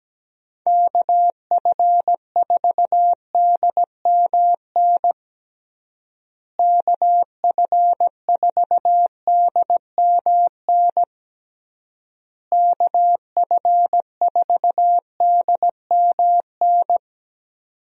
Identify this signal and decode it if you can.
Morse code, heard as a keyed tone, about 17 wpm, 700 Hz. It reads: KF4DMN KF4DMN KF4DMN
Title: USA Call Signs - 6 Characters - Repeated 3x - 17wpm